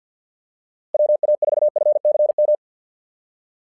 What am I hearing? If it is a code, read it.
Morse code, heard as a keyed tone, about 48 wpm, 600 Hz. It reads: WA4FCM
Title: SOTA QSO Practice 50wpm